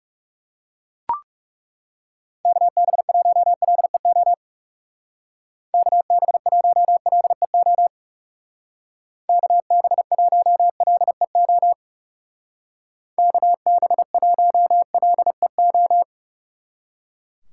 Morse code, heard as a keyed tone, about 36 wpm, 700 Hz. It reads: KB1LEO KB1LEO KB1LEO KB1LEO
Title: Speed Racing - USA 2x3 Call Signs 30wpm